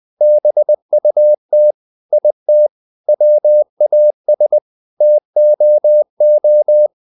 Morse code, heard as a keyed tone, about 20 wpm, 600 Hz. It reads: BUT IT WAS TOO
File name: the old man and the sea part 2 20wpm